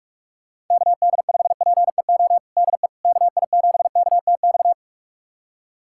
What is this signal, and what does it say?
Morse code, heard as a keyed tone, about 46 wpm, 700 Hz. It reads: KD5PEO DE KI7KTX
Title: USA Callsign DE Callsign - 6 Characters - 45wpm